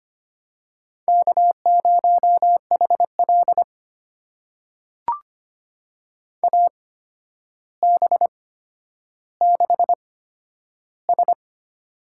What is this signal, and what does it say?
Morse code, heard as a keyed tone, about 25 wpm, spaced as for 11 wpm, 700 Hz. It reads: K0HL AB6S
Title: USA Call Signs - 4 Characters - 8x Character Spacing - 25wpm